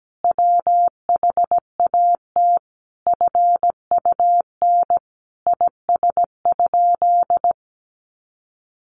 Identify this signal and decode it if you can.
Morse code, heard as a keyed tone, about 17 wpm, 700 Hz. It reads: WHAT FUN IS?
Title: Sentences from Top 700 Words 17wpm